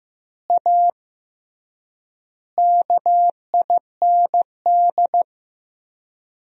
Morse code, heard as a keyed tone, about 15 wpm, 700 Hz. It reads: A KIND
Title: 3x Word-Spacing - Sentences from Top 500 Words - 15wpm